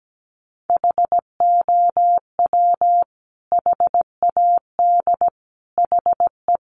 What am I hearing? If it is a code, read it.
Morse code, heard as a keyed tone, about 17 wpm, 700 Hz. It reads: HOW HAD HE
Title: Sentences from Top 1300 Words 17wpm